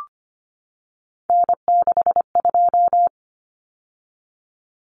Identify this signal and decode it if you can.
Morse code, heard as a keyed tone, about 25 wpm, 700 Hz. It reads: N62
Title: Common Cut Numbers - 3-Digit 25wpm